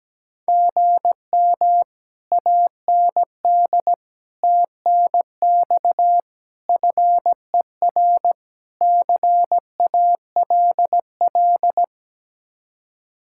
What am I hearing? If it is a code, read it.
Morse code, heard as a keyed tone, about 17 wpm, 700 Hz. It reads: GM AND TNX FER CALL
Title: Speed-Racing Common QSO Phrases Encore - 17wpm